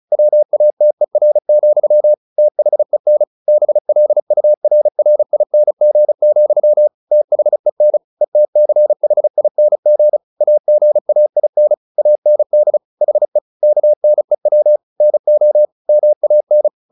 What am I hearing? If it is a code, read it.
Morse code, heard as a keyed tone, about 34 wpm, 600 Hz. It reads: WATER, THEN BLURRING, THEN ETCHING AGAIN AND HE KNEW NO MAN
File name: the old man and the sea / the old man and the sea 1 35wpm